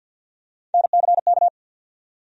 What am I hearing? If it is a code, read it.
Morse code, heard as a keyed tone, about 47 wpm, 700 Hz. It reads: NCK
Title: USA Call Sign Postfixes - 3 Letters 50wpm